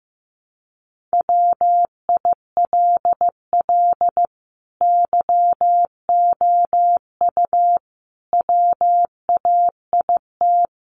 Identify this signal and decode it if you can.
Morse code, heard as a keyed tone, about 15 wpm, 700 Hz. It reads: WILL YOU WAIT
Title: Sentences from Top 900 Words 15wpm